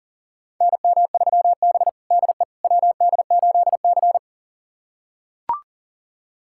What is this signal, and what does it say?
Morse code, heard as a keyed tone, about 40 wpm, 700 Hz. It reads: NM3B DE WD8C
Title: USA Callsign DE Callsign - 4 Characters - 40wpm